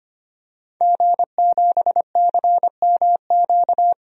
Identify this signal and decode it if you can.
Morse code, heard as a keyed tone, about 25 wpm, 700 Hz. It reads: G7CMQ
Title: DXCC-100 Call Signs - 5 Characters - Repeated 2x - 25wpm